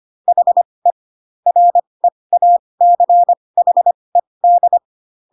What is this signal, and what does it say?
Morse code, heard as a keyed tone, about 25 wpm, spaced as for 20 wpm, 700 Hz. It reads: HE REACHED